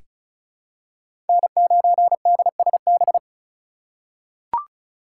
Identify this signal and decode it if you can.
Morse code, heard as a keyed tone, about 34 wpm, 700 Hz. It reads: N9DSB
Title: USA Call Signs - 5 Characters - 2x Character Spacing - 35wpm